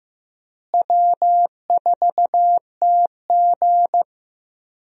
Morse code, heard as a keyed tone, about 15 wpm, 700 Hz. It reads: W4TG
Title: USA Call Signs - 4 Characters - Repeated 3x - 15wpm